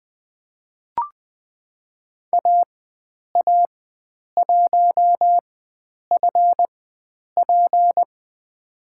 Morse code, heard as a keyed tone, about 20 wpm, spaced as for 9 wpm, 700 Hz. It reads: AA1FP
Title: USA Call Signs - 5 Characters - 4x Character Spacing - 20wpm